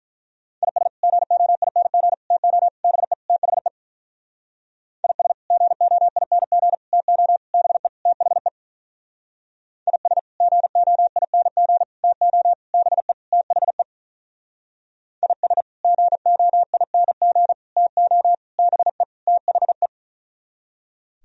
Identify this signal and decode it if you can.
Morse code, heard as a keyed tone, about 46 wpm, 700 Hz. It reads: IS GOING TO BE THE IS GOING TO BE THE IS GOING TO BE THE IS GOING TO BE THE